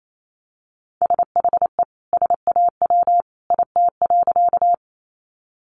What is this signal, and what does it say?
Morse code, heard as a keyed tone, about 28 wpm, 700 Hz. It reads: SHE SAW IT.
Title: Sentences from Top 300 Words 28wpm